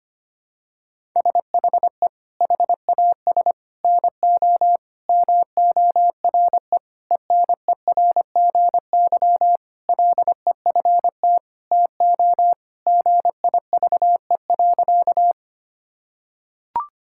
Morse code, heard as a keyed tone, about 25 wpm, 700 Hz. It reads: SHE HAS NO MORE ENERGY LEFT TO GIVE.
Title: Sentences from Top 1000 Words - Review Part 6 - 25wpm